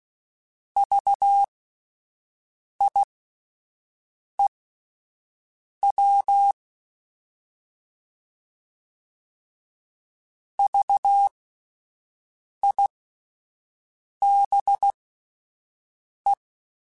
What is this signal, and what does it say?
Morse code, heard as a keyed tone, about 16 wpm, spaced as for 5 wpm, 800 Hz. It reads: VIEW VIBE
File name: Alphabetical / Alphabetical mp3s / Vs @5